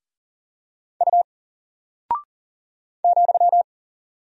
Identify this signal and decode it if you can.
Morse code, heard as a keyed tone, about 40 wpm, 700 Hz. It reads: U ,
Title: Single Character (Letter:Number:Punctuation:Prosign) 40wpm